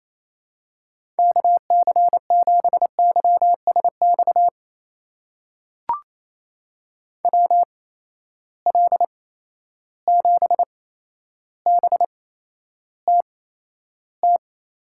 Morse code, heard as a keyed tone, about 28 wpm, spaced as for 12 wpm, 700 Hz. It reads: KC7YSX WL7BTT
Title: USA Call Signs - 6 Characters - 8x Character Spacing - 28wpm